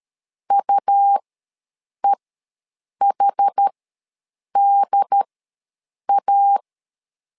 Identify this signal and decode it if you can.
Morse code, heard as a keyed tone, about 13 wpm, spaced as for 7 wpm, 800 Hz. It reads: UEHDA
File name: LEZIONI / 2 UHDN / 2 UHD ws 12wpm 1722